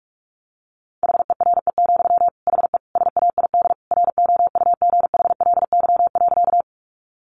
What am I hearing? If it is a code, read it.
Morse code, heard as a keyed tone, about 44 wpm, 700 Hz. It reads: HERE, HE SAID ROUGHLY.
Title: Sentences from Top 1500 Words 45wpm